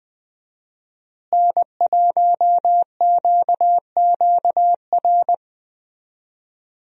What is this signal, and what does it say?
Morse code, heard as a keyed tone, about 20 wpm, 700 Hz. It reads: N1QQR
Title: USA Call Signs - 5 Characters - Repeated 2x - 20wpm